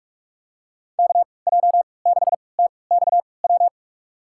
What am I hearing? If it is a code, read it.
Morse code, heard as a keyed tone, about 46 wpm, spaced as for 26 wpm, 700 Hz. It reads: KJ6TXW
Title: USA Call Signs - 3x Character Spacing - 45wpm